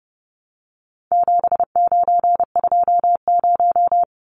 Morse code, heard as a keyed tone, about 29 wpm, 700 Hz. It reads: 7920
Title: Numbers - 4-Digit 30wpm